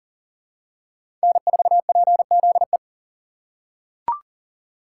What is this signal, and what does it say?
Morse code, heard as a keyed tone, about 40 wpm, 700 Hz. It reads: N4PZE
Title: USA Call Signs - 5 Characters - 6x Character Spacing - 40wpm